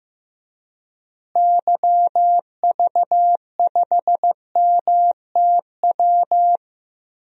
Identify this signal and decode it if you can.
Morse code, heard as a keyed tone, about 15 wpm, 700 Hz. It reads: YV5MTW
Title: DXCC-100 Call Signs - 6 Characters - Repeated 2x - 15wpm